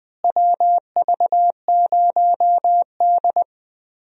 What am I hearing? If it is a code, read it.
Morse code, heard as a keyed tone, about 20 wpm, 700 Hz. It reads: WV0D
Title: USA Call Signs - 4 Characters - Repeated 3x - 20wpm